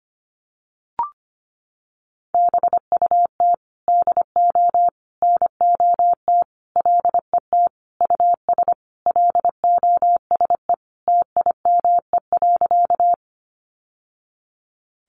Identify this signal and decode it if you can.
Morse code, heard as a keyed tone, about 25 wpm, 700 Hz. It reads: BUT DO NOT LET US LOSE TIME.
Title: Sentences from Top 1100 Words 25wpm